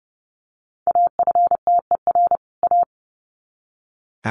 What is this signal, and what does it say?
Morse code, heard as a keyed tone, about 30 wpm, 700 Hz. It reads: AFTER A
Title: Sets of 2 Words 30wpm